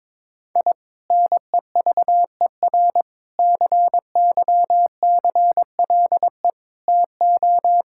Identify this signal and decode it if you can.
Morse code, heard as a keyed tone, about 22 wpm, 700 Hz. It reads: I NEVER CYCLE TO
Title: Sentences from Top 1000 Words - Review Part 6 - 22wpm